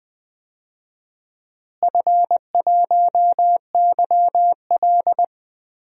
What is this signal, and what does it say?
Morse code, heard as a keyed tone, about 20 wpm, 700 Hz. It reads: F1YL
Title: DXCC-100 Call Signs - 4 Characters - Repeated 3x - 20wpm